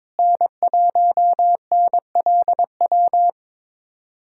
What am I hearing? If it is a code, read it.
Morse code, heard as a keyed tone, about 22 wpm, 700 Hz. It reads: N1NLW
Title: USA Callsign DE Callsign - 5 Characters - 22wpm